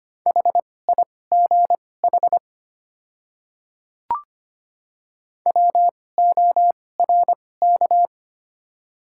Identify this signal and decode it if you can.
Morse code, heard as a keyed tone, about 25 wpm, spaced as for 18 wpm, 700 Hz. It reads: HIGH WORK
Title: 2x Character Spacing - Top 200 Words - 25wpm